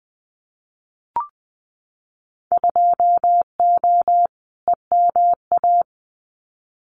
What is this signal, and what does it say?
Morse code, heard as a keyed tone, about 20 wpm, 700 Hz. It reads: 2O EMA
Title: Winter Field Day Exchanges Encore 20wpm